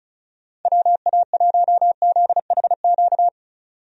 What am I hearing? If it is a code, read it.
Morse code, heard as a keyed tone, about 36 wpm, 700 Hz. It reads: WA1ZHQ
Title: USA Call Signs - 6 Characters - 35wpm